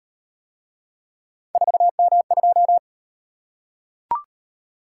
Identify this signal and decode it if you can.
Morse code, heard as a keyed tone, about 38 wpm, 700 Hz. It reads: 4M2